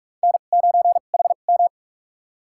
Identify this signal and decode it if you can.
Morse code, heard as a keyed tone, about 46 wpm, spaced as for 33 wpm, 700 Hz. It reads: N9HM
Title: USA Call Signs - 4 Characters - 2x Character Spacing - 45wpm